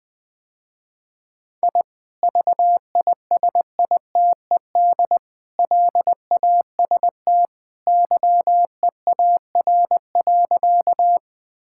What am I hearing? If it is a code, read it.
Morse code, heard as a keyed tone, about 20 wpm, 700 Hz. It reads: I VISITED LAST YEAR.
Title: Sentences with 2-Word Phrases - Spoken Beginning - 20wpm